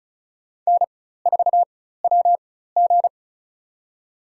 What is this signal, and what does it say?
Morse code, heard as a keyed tone, about 36 wpm, spaced as for 16 wpm, 700 Hz. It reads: N4WG
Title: USA Call Signs - 4 Characters - 4x Character Spacing - 35wpm